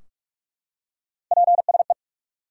Morse code, heard as a keyed tone, about 46 wpm, 700 Hz. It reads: PSE
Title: Top 180 QSO Elements 45wpm